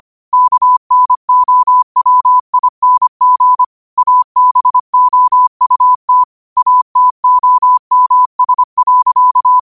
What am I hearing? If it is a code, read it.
Morse code, heard as a keyed tone, about 25 wpm, 1000 Hz. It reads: KNOWING ABOUT ATOMS.
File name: qotdfiles / 25wpm